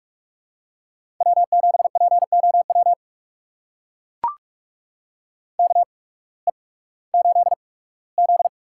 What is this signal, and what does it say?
Morse code, heard as a keyed tone, about 44 wpm, spaced as for 21 wpm, 700 Hz. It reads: W7POW KE8Z
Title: USA Call Signs - 8x Character Spacing - 45wpm